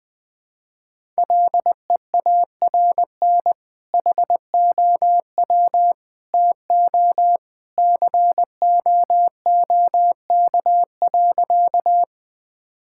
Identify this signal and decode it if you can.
Morse code, heard as a keyed tone, about 20 wpm, 700 Hz. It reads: LEARN HOW TO COOK.
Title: Sentences with 2-Word Phrases - 20wpm